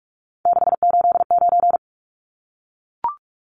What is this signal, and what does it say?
Morse code, heard as a keyed tone, about 44 wpm, 700 Hz. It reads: 689
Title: Numbers - 3-Digit 45wpm